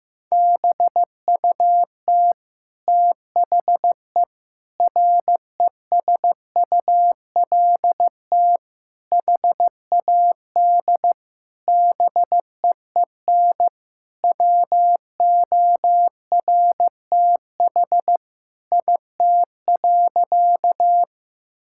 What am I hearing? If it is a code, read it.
Morse code, heard as a keyed tone, about 15 wpm, 700 Hz. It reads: BUT THE RESULT HAD BEEN WORTH IT.
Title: Sentences from Top 1000 Words - Review Part 5 - 15wpm